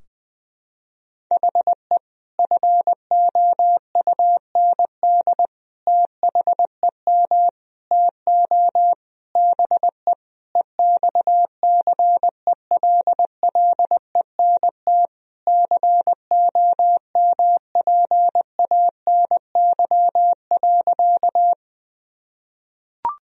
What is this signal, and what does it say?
Morse code, heard as a keyed tone, about 20 wpm, 700 Hz. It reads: HE FOUND THEM TO BE EXCELLENT COMPANY.
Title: Sentences from Top 1000 Words - Review Part 3 - 20wpm